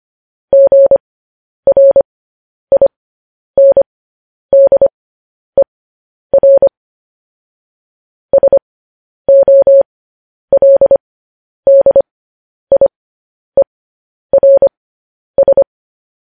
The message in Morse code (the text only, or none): GRINDER SOLDIERS